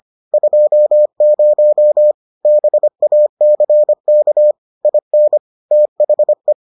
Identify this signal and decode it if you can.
Morse code, heard as a keyed tone, about 25 wpm, 600 Hz. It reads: 20 BACK IN THE